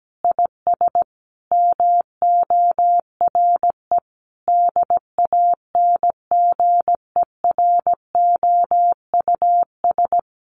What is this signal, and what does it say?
Morse code, heard as a keyed tone, about 17 wpm, 700 Hz. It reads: IS MORE DANGEROUS
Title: Sentences from Top 1200 Words 17wpm